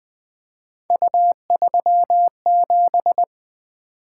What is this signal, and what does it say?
Morse code, heard as a keyed tone, about 20 wpm, 700 Hz. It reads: U37
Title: Three Letter-Number Combinations - 20wpm